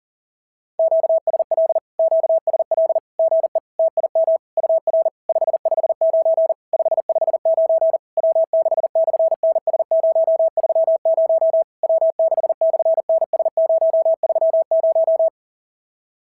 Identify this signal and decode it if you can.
Morse code, heard as a keyed tone, about 34 wpm, 660 Hz. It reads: QSL QSL GE TIM UR 559 559 W6/NS030 W6/NS030
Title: SOTA QSO Practice 40wpm